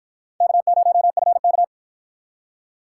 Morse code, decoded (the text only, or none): K0FK